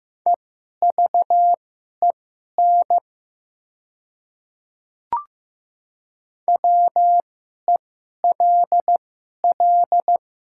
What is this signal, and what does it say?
Morse code, heard as a keyed tone, about 15 wpm, spaced as for 11 wpm, 700 Hz. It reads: EVEN WELL
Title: 2x Character Spacing - Top 200 Words - 15wpm